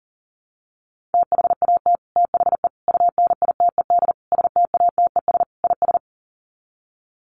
Morse code, heard as a keyed tone, about 40 wpm, 700 Hz. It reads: THAT THE UNITED STATES IS
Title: Sets of 5 Words Encore 40wpm